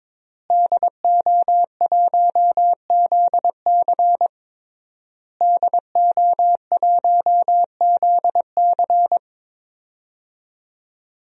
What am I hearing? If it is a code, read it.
Morse code, heard as a keyed tone, about 22 wpm, 700 Hz. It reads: DO1ZC DO1ZC
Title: DXCC-100 Call Signs - 5 Characters - Repeated 3x - 22wpm